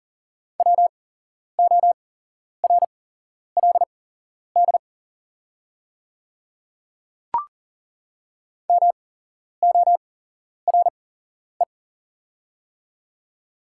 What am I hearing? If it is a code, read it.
Morse code, heard as a keyed tone, about 40 wpm, spaced as for 11 wpm, 700 Hz. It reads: WORLD MORE
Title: 8x Character Spacing - Top 500 Words - 40wpm